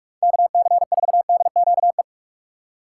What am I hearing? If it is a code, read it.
Morse code, heard as a keyed tone, about 44 wpm, 700 Hz. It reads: KC4DQE